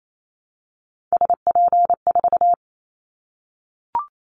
Morse code, heard as a keyed tone, about 28 wpm, 700 Hz. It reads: SP4
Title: Callsign Prefixes for Top 25 DXCC - 28wpm